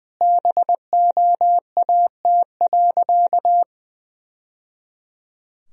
Morse code, heard as a keyed tone, about 20 wpm, 700 Hz. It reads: BOAT.
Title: Sentences from Top 1000 Words - Review Part 4 - 20wpm